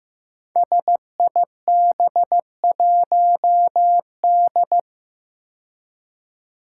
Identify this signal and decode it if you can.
Morse code, heard as a keyed tone, about 15 wpm, 700 Hz. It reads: SIB1D
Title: Five Challenging Letter-Number Combinations 15wpm